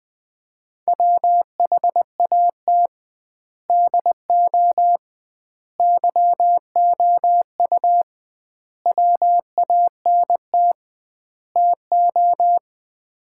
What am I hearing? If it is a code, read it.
Morse code, heard as a keyed tone, about 21 wpm, 700 Hz. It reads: WHAT DO YOU WANT TO
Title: 2x Word-Spacing - Sets of 5 Words - 20wpm